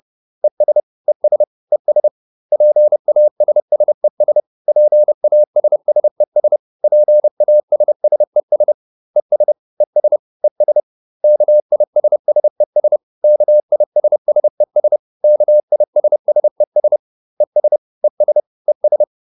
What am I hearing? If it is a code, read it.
Morse code, heard as a keyed tone, about 30 wpm, 600 Hz. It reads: ES ES ES PASSES PASSES PASSES ES ES ES KISSES KISSES KISSES ES ES ES